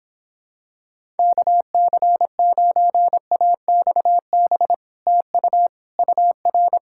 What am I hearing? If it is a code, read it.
Morse code, heard as a keyed tone, about 26 wpm, 700 Hz. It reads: KC9AXB TU UR